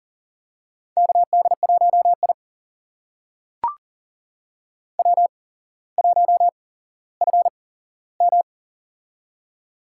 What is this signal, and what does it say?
Morse code, heard as a keyed tone, about 40 wpm, spaced as for 17 wpm, 700 Hz. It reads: KD1I W1FM
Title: USA Call Signs - 4 Characters - 8x Character Spacing - 40wpm